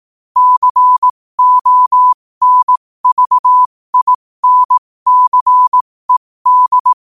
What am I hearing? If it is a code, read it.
Morse code, heard as a keyed tone, about 18 wpm, spaced as for 15 wpm, 1000 Hz. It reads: CONVINCED
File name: qotdfiles / 15wpm